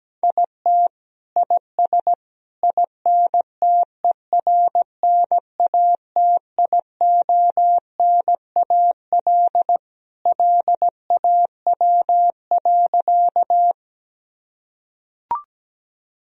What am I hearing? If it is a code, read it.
Morse code, heard as a keyed tone, about 17 wpm, 700 Hz. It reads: IT IS INTERNATIONAL LAW.
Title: Sentences from Top 1000 Words - Review Part 4 - 17wpm